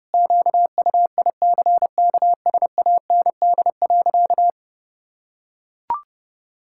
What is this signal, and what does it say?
Morse code, heard as a keyed tone, about 32 wpm, 700 Hz. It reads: QUICKSAND.
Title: Pangram Challenge v2 - 30wpm